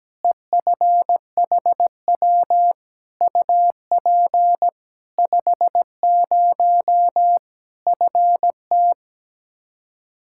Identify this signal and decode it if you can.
Morse code, heard as a keyed tone, about 17 wpm, 700 Hz. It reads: EFHW UP 50 FT
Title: Practice QSO - Exchange 2 - Info - 17wpm